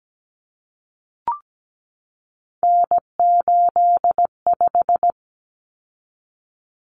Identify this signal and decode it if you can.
Morse code, heard as a keyed tone, about 17 wpm, 700 Hz. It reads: N85
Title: Common Cut Numbers - 3-Digit 17wpm